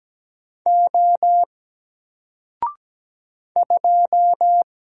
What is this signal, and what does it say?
Morse code, heard as a keyed tone, about 17 wpm, 700 Hz. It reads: O 2
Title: Single Letter-Number 17wpm